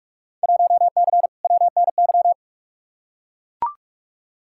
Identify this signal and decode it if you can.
Morse code, heard as a keyed tone, about 42 wpm, 700 Hz. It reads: 1C WNY
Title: Field Day Exchanges Encore - 45wpm